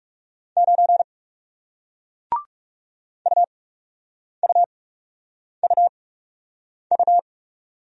Morse code, heard as a keyed tone, about 41 wpm, spaced as for 8 wpm, 700 Hz. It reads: 9 UUUU